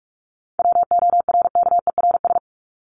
Morse code, heard as a keyed tone, about 44 wpm, 700 Hz. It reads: WORKERS
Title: ES:S Suffix - Top 100 Words with Base Words 45wpm